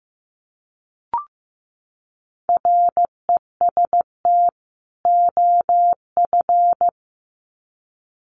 Morse code, heard as a keyed tone, about 15 wpm, 700 Hz. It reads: REST OF
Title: Sets of 2 Words 15wpm